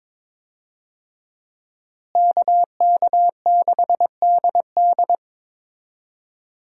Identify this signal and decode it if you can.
Morse code, heard as a keyed tone, about 22 wpm, 700 Hz. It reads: KK6DD
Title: USA Call Signs - 5 Characters - Repeated 2x - 22wpm